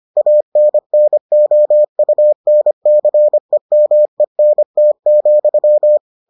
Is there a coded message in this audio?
Yes: Morse code from 0.2 to 6.0 s, heard as a keyed tone, about 25 wpm, 600 Hz. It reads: ANNOUNCEMENT,